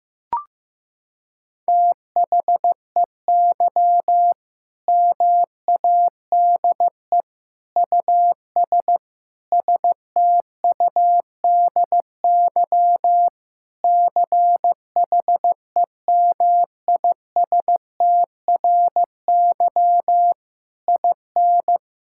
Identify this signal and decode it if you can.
Morse code, heard as a keyed tone, about 16 wpm, 700 Hz. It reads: THEY MADE US STUDY CHEMISTRY IN
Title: Sentences from Top 1000 Words - Review Part 6 - 15wpm